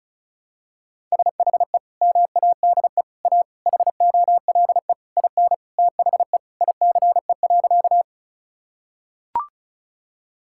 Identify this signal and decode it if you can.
Morse code, heard as a keyed tone, about 36 wpm, 700 Hz. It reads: SHE MADE A HOLE IN THE ICE.
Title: Sentences from Top 1600 Words 35wpm